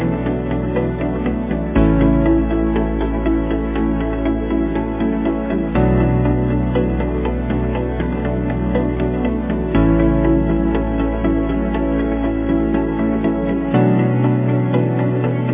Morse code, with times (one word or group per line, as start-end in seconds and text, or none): none